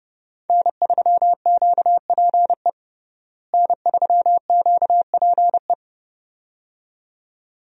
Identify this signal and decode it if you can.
Morse code, heard as a keyed tone, about 30 wpm, 700 Hz. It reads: N3QPE N3QPE
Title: USA Call Signs - 5 Characters - Repeated 2x - 30wpm